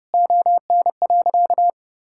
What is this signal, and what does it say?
Morse code, heard as a keyed tone, about 32 wpm, 700 Hz. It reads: ON.
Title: Sentences from Top 1000 Words - Review Part 6 - 30wpm